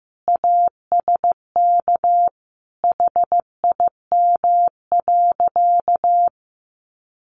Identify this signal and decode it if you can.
Morse code, heard as a keyed tone, about 15 wpm, 700 Hz. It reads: ASK HIM.
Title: Sentences from Top 1000 Words 15wpm